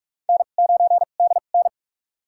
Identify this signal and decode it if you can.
Morse code, heard as a keyed tone, about 46 wpm, spaced as for 33 wpm, 700 Hz. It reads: N9DN
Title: USA Call Signs - 2x Character Spacing - 45wpm